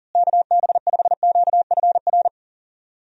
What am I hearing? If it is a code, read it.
Morse code, heard as a keyed tone, about 40 wpm, 700 Hz. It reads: KB5QFR
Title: USA Call Signs - 6 Characters - Repeated 3x - 40wpm